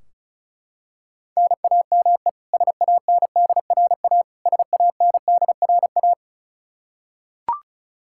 Morse code, heard as a keyed tone, about 36 wpm, 700 Hz. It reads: NAME SANDRA SANDRA
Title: Practice QSO - Exchange 1 - Beginning - 35wpm